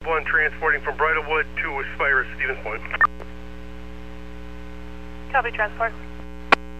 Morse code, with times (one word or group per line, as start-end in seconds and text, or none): none